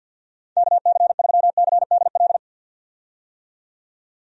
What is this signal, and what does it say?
Morse code, heard as a keyed tone, about 51 wpm, 700 Hz. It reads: KC3CDL
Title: USA Call Signs - 6 Characters - 50wpm